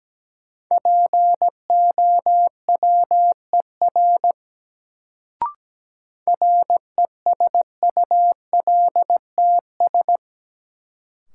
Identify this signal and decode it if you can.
Morse code, heard as a keyed tone, about 17 wpm, 700 Hz. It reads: POWER RESULTS
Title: Sets of 4 Words - New Word Drill - 17wpm